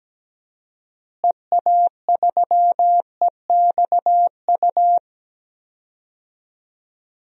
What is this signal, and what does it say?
Morse code, heard as a keyed tone, about 17 wpm, 700 Hz. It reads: EA3EXU
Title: DXCC-100 Call Signs - 6 Characters - Repeated 2x - 17wpm